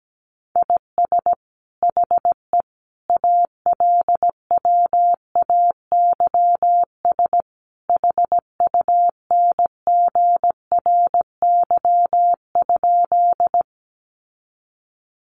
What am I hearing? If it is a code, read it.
Morse code, heard as a keyed tone, about 17 wpm, 700 Hz. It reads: IS HE ALWAYS HUNGRY?
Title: Sentences from Top 1400 Words 17wpm